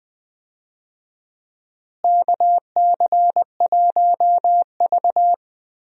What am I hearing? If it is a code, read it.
Morse code, heard as a keyed tone, about 20 wpm, 700 Hz. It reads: KC1V
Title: USA Call Signs - 4 Characters - Repeated 2x - 20wpm